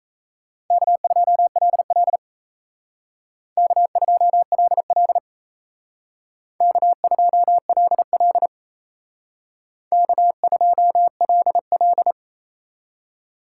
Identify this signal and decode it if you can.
Morse code, heard as a keyed tone, about 32 wpm, 700 Hz. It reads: K2LL K2LL K2LL K2LL
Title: Speed Racing - USA 1x2 Call Signs 28wpm